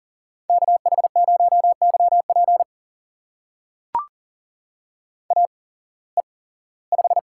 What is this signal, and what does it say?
Morse code, heard as a keyed tone, about 40 wpm, spaced as for 21 wpm, 700 Hz. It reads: KH0YP AE5
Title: USA Call Signs - 5 Characters - 8x Character Spacing - 40wpm